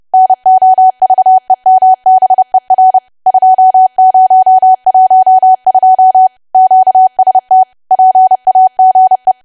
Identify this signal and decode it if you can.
Morse code, heard as a keyed tone, about 30 wpm, 750 Hz. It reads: NOVEMBER 2012 QST PAGE
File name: cwa / W1AW-Level3 / 30WPM